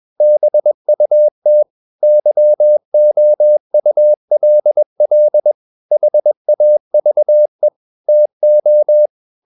Morse code, heard as a keyed tone, about 21 wpm, 600 Hz. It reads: BUT YOULL HAVE TO
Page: CW med Gnister